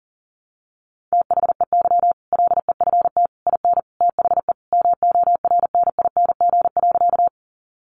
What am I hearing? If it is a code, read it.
Morse code, heard as a keyed tone, about 40 wpm, 700 Hz. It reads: THEY LEFT IN THE MORNING.
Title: Sentences from Top 400 Words 40wpm